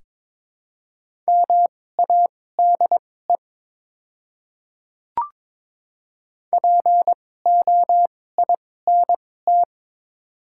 Morse code, heard as a keyed tone, about 22 wpm, spaced as for 16 wpm, 700 Hz. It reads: MADE POINT
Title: 2x Character Spacing - Top 200 Words - 22wpm